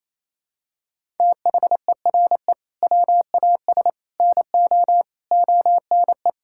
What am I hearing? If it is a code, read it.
Morse code, heard as a keyed tone, about 27 wpm, 700 Hz. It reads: THERE WAS NO ONE
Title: Sentences from Top 1600 Words 28wpm